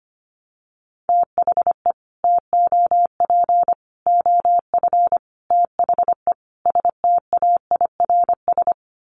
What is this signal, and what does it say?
Morse code, heard as a keyed tone, about 25 wpm, 700 Hz. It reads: THE TOP OF THE STAIRS
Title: Sets of 5 Words Encore 25wpm